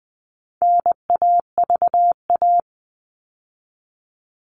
Morse code, heard as a keyed tone, about 20 wpm, 700 Hz. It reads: NAVA
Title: All Cut Numbers - 4-Digit 20wpm